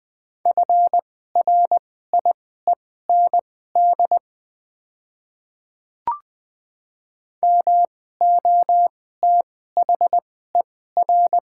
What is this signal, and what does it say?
Morse code, heard as a keyed tone, about 20 wpm, spaced as for 14 wpm, 700 Hz. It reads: FRIEND MOTHER
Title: 2x Character Spacing - Top 500 Words - 20wpm